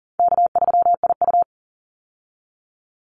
Morse code, heard as a keyed tone, about 37 wpm, 700 Hz. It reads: K3IU
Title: Easy Call Signs Repeated 2x - 40wpm